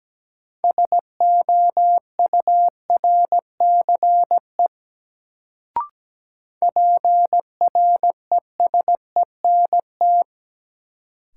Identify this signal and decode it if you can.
Morse code, heard as a keyed tone, about 17 wpm, 700 Hz. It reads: SOURCE PRESENT
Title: Sentences from Top 800 Words - New Word Drill - 17wpm